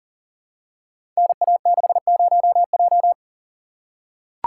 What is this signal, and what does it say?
Morse code, heard as a keyed tone, about 38 wpm, 700 Hz. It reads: NA60J
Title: Code Groups - 5 Letter:Numbers - 40wpm